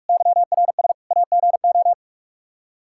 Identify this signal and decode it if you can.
Morse code, heard as a keyed tone, about 46 wpm, 700 Hz. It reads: YRS AGO
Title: Practice QSO - Exchange 3 - More Info - 45wpm